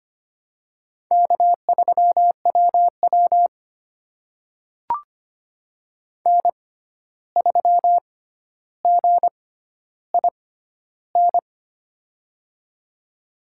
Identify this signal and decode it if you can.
Morse code, heard as a keyed tone, about 25 wpm, spaced as for 12 wpm, 700 Hz. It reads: K3WW N3GIN
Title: USA Call Signs - 6x Character Spacing - 25wpm